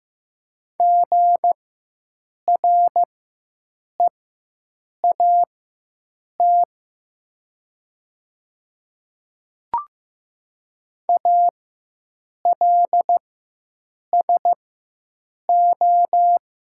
Morse code, heard as a keyed tone, about 15 wpm, spaced as for 7 wpm, 700 Hz. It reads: GREAT ALSO
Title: 4x Character Spacing - Top 200 Words - 15wpm